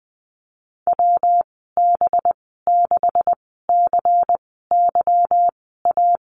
Morse code, heard as a keyed tone, about 20 wpm, spaced as for 14 wpm, 700 Hz. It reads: WB6CYA
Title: 186 Lesson 30g - Letter b - Call Signs 20wpm